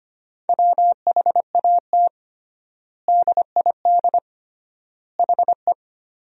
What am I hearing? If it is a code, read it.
Morse code, heard as a keyed tone, about 25 wpm, 700 Hz. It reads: WHAT DID HE
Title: 3x Word-Spacing - Sentences from Top 300 Words - 25wpm